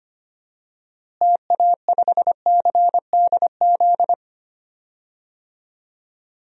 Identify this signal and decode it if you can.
Morse code, heard as a keyed tone, about 25 wpm, 700 Hz. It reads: TA5CDZ
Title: DXCC-100 Call Signs - 6 Characters - Repeated 2x - 25wpm